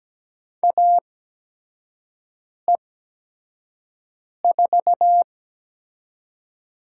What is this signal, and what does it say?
Morse code, heard as a keyed tone, about 17 wpm, spaced as for 5 wpm, 700 Hz. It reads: AE4
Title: USA Call Signs - 5 Characters - 8x Character Spacing - 17wpm